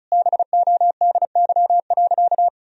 Morse code, heard as a keyed tone, about 36 wpm, 700 Hz. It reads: BODY.
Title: Sentences from Top 1000 Words - Review Part 1 - 35wpm